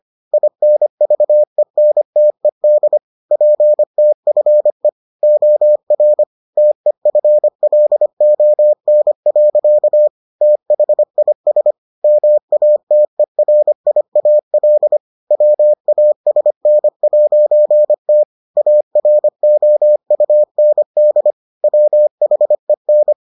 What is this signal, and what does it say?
Morse code, heard as a keyed tone, about 25 wpm, 600 Hz. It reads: INVENTED PTFE OR TEFLON. THIS MATERIAL WASN'T AROUND WHEN